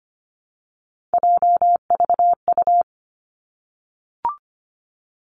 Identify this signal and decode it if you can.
Morse code, heard as a keyed tone, about 25 wpm, 700 Hz. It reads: JVU
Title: Three Challenging Letter-Number Combinations 25wpm